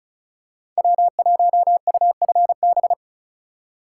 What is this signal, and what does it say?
Morse code, heard as a keyed tone, about 36 wpm, 700 Hz. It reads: W1UFB
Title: USA Call Signs - 4x Character Spacing - 35wpm